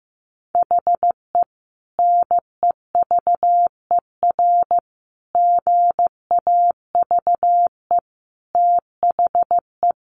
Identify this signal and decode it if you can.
Morse code, heard as a keyed tone, about 15 wpm, 700 Hz. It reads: HE NEVER GAVE THE
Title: Sentences from Top 1200 Words 15wpm